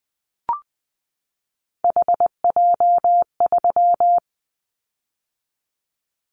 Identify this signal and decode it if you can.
Morse code, heard as a keyed tone, about 20 wpm, 700 Hz. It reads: HJ3
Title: Callsign Prefixes for Top 100 DXCC - 20wpm